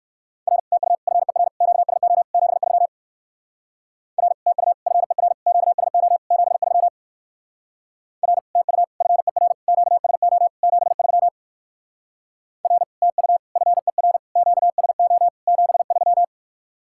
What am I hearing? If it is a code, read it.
Morse code, heard as a keyed tone, about 59 wpm, 700 Hz. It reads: R TU FER QSO 73 R TU FER QSO 73 R TU FER QSO 73 R TU FER QSO 73